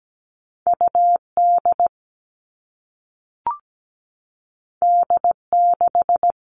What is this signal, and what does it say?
Morse code, heard as a keyed tone, about 17 wpm, 700 Hz. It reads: UD D6
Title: All Cut Numbers - 2-Digit 17wpm